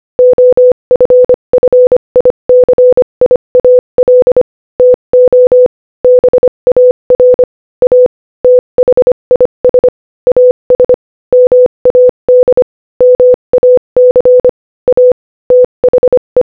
Morse code, heard as a keyed tone, about 25 wpm, 500 Hz. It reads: OFFICIAL TO BAR A THIS AS MAD MAC A THE
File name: Morse-Code-Audio-Speed-25-5